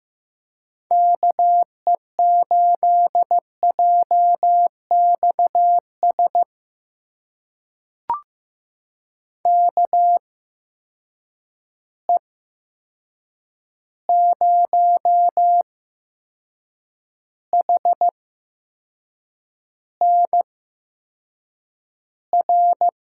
Morse code, heard as a keyed tone, about 15 wpm, spaced as for 6 wpm, 700 Hz. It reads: KE8JXS KE0HNR
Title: USA Call Signs - 6 Characters - 8x Character Spacing - 15wpm